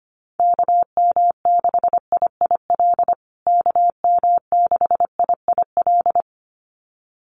Morse code, heard as a keyed tone, about 25 wpm, 700 Hz. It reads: KM6IIL KM6IIL
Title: USA Call Signs Repeated 2x 25wpm